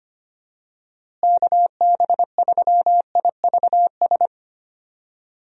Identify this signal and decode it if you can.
Morse code, heard as a keyed tone, about 25 wpm, 700 Hz. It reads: KB3IVS
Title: USA Call Signs - 6 Characters - Repeated 2x - 25wpm